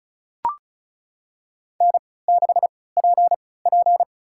Morse code, heard as a keyed tone, about 36 wpm, spaced as for 20 wpm, 700 Hz. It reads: N6PP
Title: USA Call Signs - 4 Characters - 3x Character Spacing - 35wpm